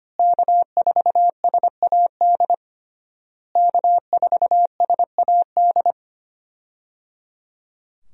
Morse code, heard as a keyed tone, about 25 wpm, 700 Hz. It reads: K4SAD K4SAD
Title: USA Call Signs - 5 Characters - Repeated 3x - 25wpm